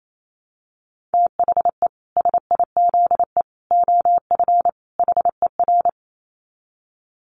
Morse code, heard as a keyed tone, about 28 wpm, 700 Hz. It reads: THE SIZE OF HER
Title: Dit Challenge 28wpm